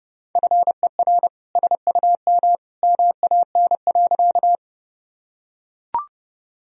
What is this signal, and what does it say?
Morse code, heard as a keyed tone, about 30 wpm, 700 Hz. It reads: FER SUM MAN.
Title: Sentences from Top 200 Words and Abbreviations 30wpm